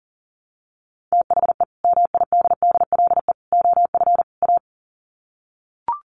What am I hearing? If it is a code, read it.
Morse code, heard as a keyed tone, about 37 wpm, 700 Hz. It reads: THE MIDDLE OF A
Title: Sets of 4 Words 40wpm